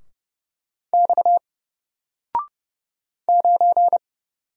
Morse code, heard as a keyed tone, about 29 wpm, 700 Hz. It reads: X 9
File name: Single-Letter-Number-30wpm